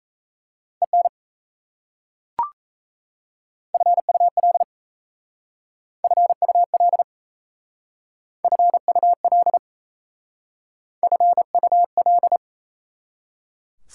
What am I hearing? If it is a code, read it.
Morse code, heard as a keyed tone, about 34 wpm, 690 Hz. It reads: EN FUL FUL FUL FUL
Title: Speed-Racing Prefixes and Suffixes 28wpm